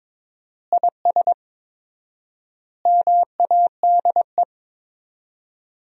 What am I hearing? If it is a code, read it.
Morse code, heard as a keyed tone, about 23 wpm, 700 Hz. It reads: IS MADE
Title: 4x Word-Spacing - Sentences from Top 500 Words - 22wpm